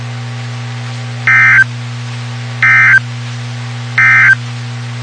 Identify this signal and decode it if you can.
Emergency Alert System (SAME) tones, heard as harsh buzzing bursts. It is heard from 1.2 to 4.4 s.
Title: WXK91 NOAA Weather Radio